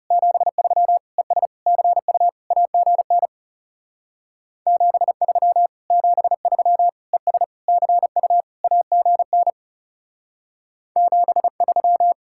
Morse code, heard as a keyed tone, about 34 wpm, 700 Hz. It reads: 73 ES CU AGN 73 73 ES CU AGN 73
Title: Speed-Racing Common QSO Phrases - 30wpm